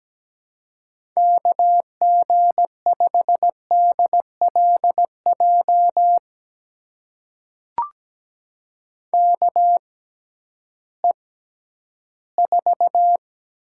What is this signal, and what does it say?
Morse code, heard as a keyed tone, about 17 wpm, spaced as for 11 wpm, 700 Hz. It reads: KG5DLJ KE4
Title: USA Call Signs - 6x Character Spacing - 17wpm